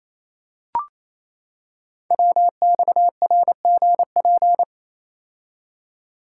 Morse code, heard as a keyed tone, about 29 wpm, 700 Hz. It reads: WXRGP
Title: Code Groups - 5 Letters - 28wpm